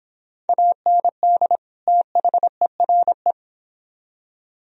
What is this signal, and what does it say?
Morse code, heard as a keyed tone, about 31 wpm, 700 Hz. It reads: AND THERE